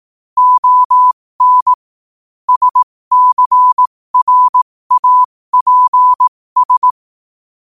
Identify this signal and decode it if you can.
Morse code, heard as a keyed tone, about 18 wpm, spaced as for 15 wpm, 1000 Hz. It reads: ON SCRAPS
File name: qotdfiles / 15wpm